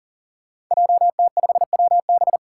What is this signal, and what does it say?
Morse code, heard as a keyed tone, about 40 wpm, 700 Hz. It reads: JT5WB
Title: Five Challenging Letter-Number Combinations 40wpm